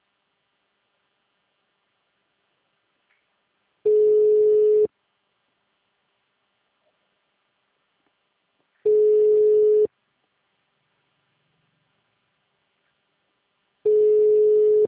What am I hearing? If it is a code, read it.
Ringback tone (Europe, 425 Hz).
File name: out- -2011- - - .671